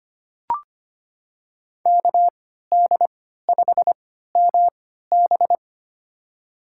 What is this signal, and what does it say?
Morse code, heard as a keyed tone, about 25 wpm, spaced as for 14 wpm, 700 Hz. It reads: KD5MB
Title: USA Call Signs - 5 Characters - 3x Character Spacing - 25wpm